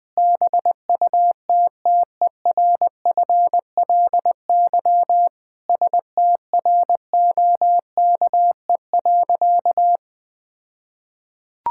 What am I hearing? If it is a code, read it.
Morse code, heard as a keyed tone, about 19 wpm, 700 Hz. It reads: BUTTERFLY STROKE.
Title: All-Age Appropriate Jokes - 20wpm